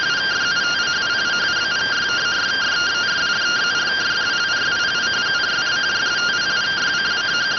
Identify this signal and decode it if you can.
Two-tone teleprinter signal, about 170 Hz shift.